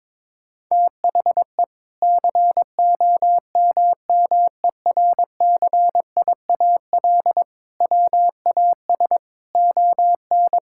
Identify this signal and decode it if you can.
Morse code, heard as a keyed tone, about 22 wpm, 700 Hz. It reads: THE COMMERCIAL WAS ON
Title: Sentences from Top 1000 Words - Review Part 5 - 22wpm